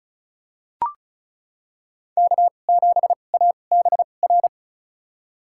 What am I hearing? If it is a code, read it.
Morse code, heard as a keyed tone, about 36 wpm, spaced as for 25 wpm, 700 Hz. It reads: K7ABR
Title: USA Call Signs - 5 Characters - 2x Character Spacing - 35wpm